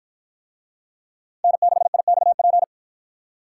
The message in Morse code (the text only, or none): N6IXP